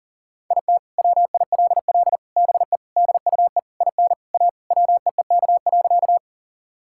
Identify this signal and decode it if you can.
Morse code, heard as a keyed tone, about 40 wpm, 700 Hz. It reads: IT WILL BE DUE IN A WEEK.
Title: Sentences from Top 1000 Words - Review Part 6 - 40wpm